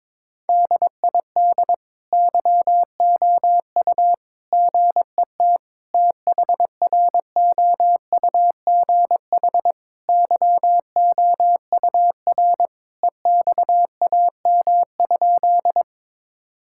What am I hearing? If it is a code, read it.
Morse code, heard as a keyed tone, about 22 wpm, 700 Hz. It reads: DID YOU GET THROUGH YOUR EXAM?
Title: Sentences from Top 1000 Words - Review Part 5 - 22wpm